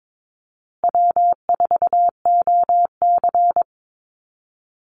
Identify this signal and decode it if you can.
Morse code, heard as a keyed tone, about 22 wpm, 700 Hz. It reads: W4OC
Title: Easy Call Signs Repeated 3x - 22wpm